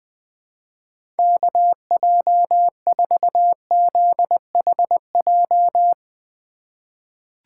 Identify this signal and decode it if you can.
Morse code, heard as a keyed tone, about 20 wpm, 700 Hz. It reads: KJ4ZHJ
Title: USA Call Signs - 6 Characters - 3x Character Spacing - 20wpm